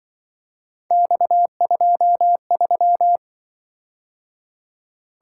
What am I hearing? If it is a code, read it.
Morse code, heard as a keyed tone, about 25 wpm, 700 Hz. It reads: X23
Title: Speed-Racing - Three Letter-Number Combinations - 20wpm